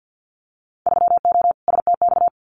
Morse code, heard as a keyed tone, about 50 wpm, 700 Hz. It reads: 3O STX
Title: Winter Field Day Exchanges 50wpm